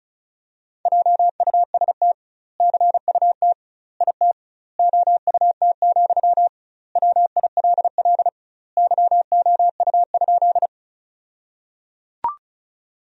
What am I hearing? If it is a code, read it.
Morse code, heard as a keyed tone, about 34 wpm, 700 Hz. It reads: JUST CUT IT OUT, WILL YOU?
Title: 2x Word-Spacing - Sentences from Top 300 Words - 35wpm